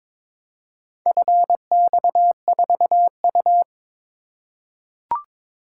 Morse code, heard as a keyed tone, about 22 wpm, 700 Hz. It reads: FX4U
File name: practice / 22wpm / Four Letter-Number Combinations - 22wpm